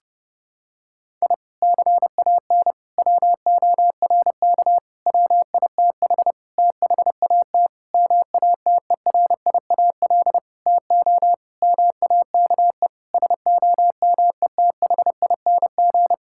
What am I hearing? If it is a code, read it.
Morse code, heard as a keyed tone, about 30 wpm, 700 Hz. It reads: I CAN WORK WITH THAT MATERIAL TO MAKE SOMETHING
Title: Sentences from Top 1000 Words - Review Part 5 - 30wpm